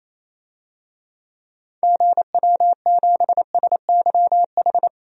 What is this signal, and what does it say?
Morse code, heard as a keyed tone, about 28 wpm, 700 Hz. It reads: GW7SYH
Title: DXCC-100 Call Signs - 6 Characters - 28wpm